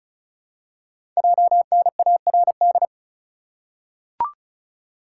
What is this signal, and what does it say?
Morse code, heard as a keyed tone, about 36 wpm, 700 Hz. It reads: JNARD
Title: Code Groups - 5 Letters - 35wpm